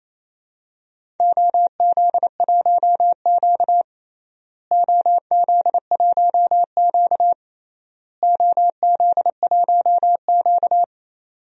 Morse code, heard as a keyed tone, about 28 wpm, 700 Hz. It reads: OZ1Q OZ1Q OZ1Q
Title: DXCC-100 Call Signs - 4 Characters - Repeated 3x - 28wpm